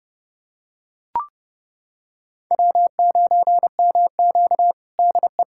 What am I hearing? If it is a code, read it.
Morse code, heard as a keyed tone, about 30 wpm, 700 Hz. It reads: W9MQ DE
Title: USA Callsign DE Callsign - 4 Characters - 30wpm